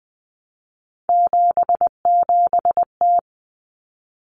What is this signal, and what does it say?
Morse code, heard as a keyed tone, about 20 wpm, 700 Hz. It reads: 77T
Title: Common Cut Numbers - 3-Digit 20wpm